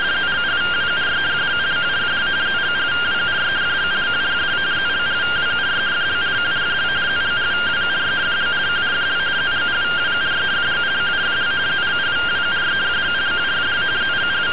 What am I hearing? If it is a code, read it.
Two-tone teleprinter signal, about 170 Hz shift.